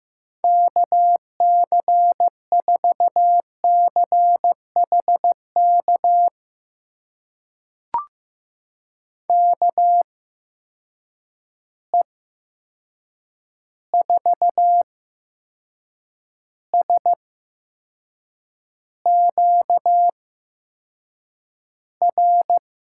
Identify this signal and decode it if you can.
Morse code, heard as a keyed tone, about 15 wpm, spaced as for 6 wpm, 700 Hz. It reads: KC4CHK KE4SQR
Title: USA Call Signs - 6 Characters - 8x Character Spacing - 15wpm